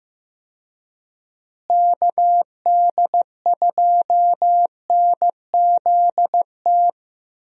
Morse code, heard as a keyed tone, about 15 wpm, 700 Hz. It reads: KD2NZT
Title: USA Call Signs - 6 Characters - Repeated 2x - 15wpm